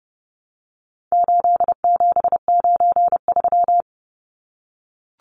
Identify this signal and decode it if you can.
Morse code, heard as a keyed tone, about 29 wpm, 700 Hz. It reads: 8793
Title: Numbers - 4-Digit 30wpm